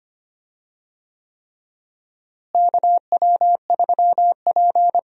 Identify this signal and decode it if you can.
Morse code, heard as a keyed tone, about 25 wpm, 700 Hz. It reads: KW3P
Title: USA Call Signs - 4 Characters - Repeated 3x - 25wpm